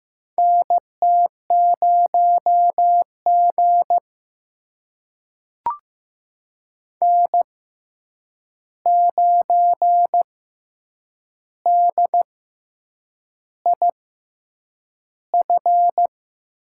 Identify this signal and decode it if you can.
Morse code, heard as a keyed tone, about 15 wpm, spaced as for 7 wpm, 700 Hz. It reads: NT0G N9DIF
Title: USA Call Signs - 6x Character Spacing - 15wpm